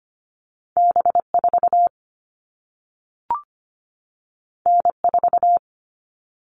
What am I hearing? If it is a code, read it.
Morse code, heard as a keyed tone, about 25 wpm, 700 Hz. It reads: B4 N4
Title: All Cut Numbers - 2-Digit 25wpm